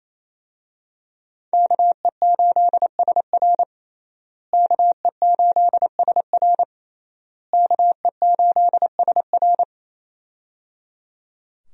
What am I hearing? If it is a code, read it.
Morse code, heard as a keyed tone, about 28 wpm, 700 Hz. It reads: KE8SR KE8SR KE8SR
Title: USA Call Signs - 5 Characters - Repeated 3x - 28wpm